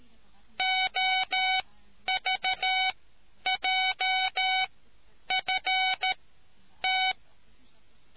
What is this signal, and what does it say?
Morse code, heard as a keyed tone, about 13 wpm, spaced as for 9 wpm, 750 Hz. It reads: OVJFT